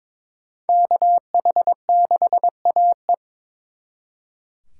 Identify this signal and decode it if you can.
Morse code, heard as a keyed tone, about 26 wpm, 700 Hz. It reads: KH6AE
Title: Speed Racing - USA 2x2 Call Signs 22wpm